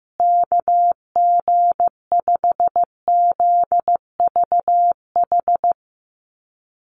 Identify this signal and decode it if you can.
Morse code, heard as a keyed tone, about 15 wpm, 700 Hz. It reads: KG5ZVH
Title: USA Call Signs Repeated 2x 15wpm